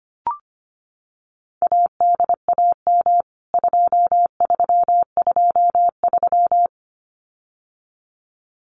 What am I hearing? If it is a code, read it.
Morse code, heard as a keyed tone, about 25 wpm, 700 Hz. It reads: ADAM 2323
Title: CWT Exchange 25wpm